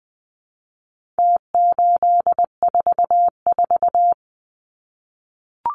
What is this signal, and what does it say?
Morse code, heard as a keyed tone, about 20 wpm, 700 Hz. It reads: T844
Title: Numbers - Contest Style - 4-Digit 20wpm